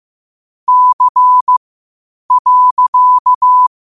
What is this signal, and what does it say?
Morse code, heard as a keyed tone, about 15 wpm, spaced as for 8 wpm, 1000 Hz. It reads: C.
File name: qotdfiles / 8wpm